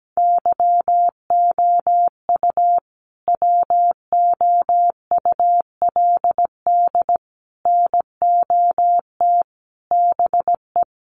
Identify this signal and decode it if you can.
Morse code, heard as a keyed tone, about 17 wpm, 700 Hz. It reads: YOU WOULD NOT BE
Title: Sentences from Top 1200 Words 17wpm